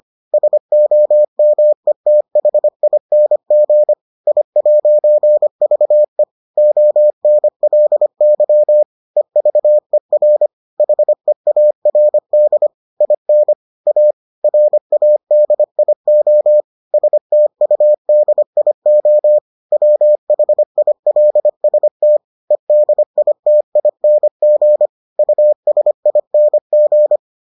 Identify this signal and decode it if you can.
Morse code, heard as a keyed tone, about 25 wpm, 600 Hz. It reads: SOMETHING I'VE ONLY EVER HEARD IN A RADIO STUDIO WHILST EDITING USING